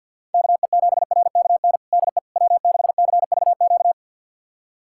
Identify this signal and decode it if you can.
Morse code, heard as a keyed tone, about 48 wpm, 700 Hz. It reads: KE7RKN DE W6CVQ
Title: USA Callsign DE Callsign 50wpm